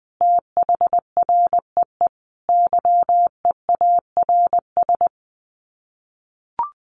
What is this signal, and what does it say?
Morse code, heard as a keyed tone, about 19 wpm, 700 Hz. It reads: THREE YEARS
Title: Sets of 5 Words Encore2 20wpm